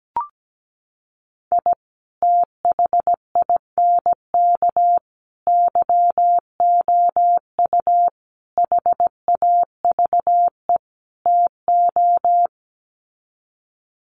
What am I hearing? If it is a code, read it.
Morse code, heard as a keyed tone, about 17 wpm, 700 Hz. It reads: I THINK YOU HAVE TO
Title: Sets of 5 Words 17wpm